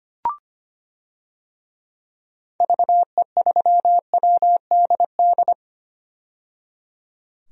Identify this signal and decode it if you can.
Morse code, heard as a keyed tone, about 25 wpm, 700 Hz. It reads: VE3WDD
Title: DXCC-100 Call Signs - 6 Characters - 25wpm